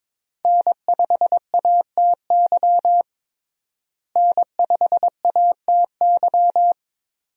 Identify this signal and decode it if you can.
Morse code, heard as a keyed tone, about 22 wpm, 700 Hz. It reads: N5ATY N5ATY
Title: USA Call Signs - 5 Characters - Repeated 2x - 22wpm